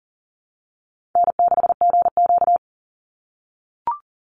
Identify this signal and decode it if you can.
Morse code, heard as a keyed tone, about 37 wpm, 700 Hz. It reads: N6GQ
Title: Easy Call Signs - 40wpm